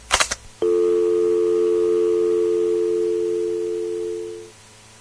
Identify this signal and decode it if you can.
Dial tone (North America), 350 + 440 Hz.